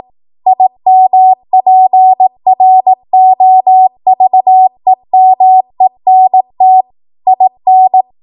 Morse code, heard as a keyed tone, about 18 wpm, 750 Hz. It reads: IMPROVEMENT IN
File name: Archive / 18 WPM / 18WPM